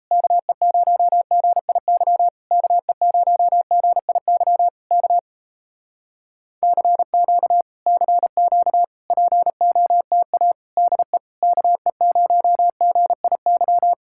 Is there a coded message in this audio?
Yes: Morse code from 0.1 to 13.9 s, heard as a keyed tone, about 38 wpm, 700 Hz. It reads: KE0GIY KE0GIY K CQ CQ POTA DE KE0GIY